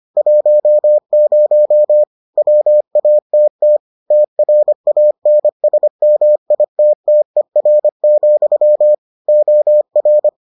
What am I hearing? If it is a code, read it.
Morse code, heard as a keyed tone, about 25 wpm, 600 Hz. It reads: 10 WATT TRANSMITTER, OR